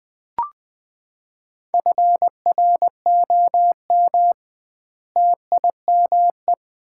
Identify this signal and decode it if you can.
Morse code, heard as a keyed tone, about 21 wpm, 700 Hz. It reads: FROM TIME
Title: 2x Word-Spacing - Sets of 4 Words - 20wpm